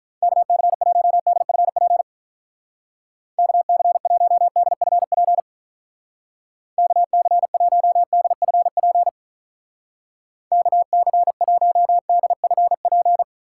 Morse code, heard as a keyed tone, about 45 wpm, 700 Hz. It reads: KC1DFP KC1DFP KC1DFP KC1DFP